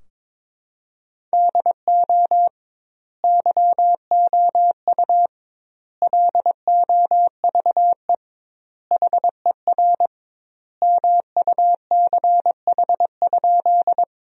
Morse code, heard as a keyed tone, about 22 wpm, 700 Hz. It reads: DO YOU LOVE HER MUCH?
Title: 2x Word-Spacing - Sentences from Top 400 Words - 22wpm